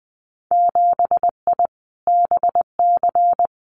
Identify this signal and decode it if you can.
Morse code, heard as a keyed tone, about 20 wpm, 700 Hz. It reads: 7I BC
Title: Winter Field Day Exchanges Encore 20wpm